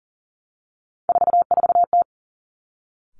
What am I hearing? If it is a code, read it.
Morse code, heard as a keyed tone, about 38 wpm, 700 Hz. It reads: 44T
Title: Three Challenging Letter-Number Combinations 40wpm